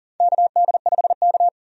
Morse code, heard as a keyed tone, about 40 wpm, 700 Hz. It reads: KD5K
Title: USA Call Signs - 4 Characters - 40wpm